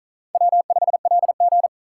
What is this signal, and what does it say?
Morse code, heard as a keyed tone, about 45 wpm, 700 Hz. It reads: W5LG